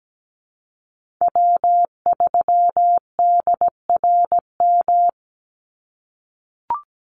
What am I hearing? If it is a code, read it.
Morse code, heard as a keyed tone, about 17 wpm, 700 Hz. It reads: W3DRM
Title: USA Call Signs 17wpm